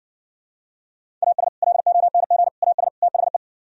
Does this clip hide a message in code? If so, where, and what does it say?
Yes: Morse code from 1.2 to 3.4 s, heard as a keyed tone, about 66 wpm, 700 Hz. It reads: AS LONG AS THE